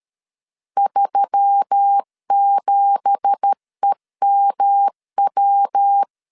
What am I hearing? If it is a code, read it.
Morse code, heard as a keyed tone, about 13 wpm, 800 Hz. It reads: 37EMW
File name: CW / DET / 1 9 1500C S C 12wpm 3811